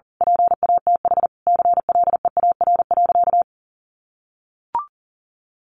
Morse code, heard as a keyed tone, about 40 wpm, 700 Hz. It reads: PATH CLEAR.
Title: Sentences from Top 1400 Words 40wpm